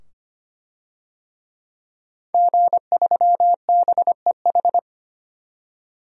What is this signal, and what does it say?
Morse code, heard as a keyed tone, about 25 wpm, 700 Hz. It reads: G3BEH
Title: DXCC-100 Call Signs - 5 Characters - 25wpm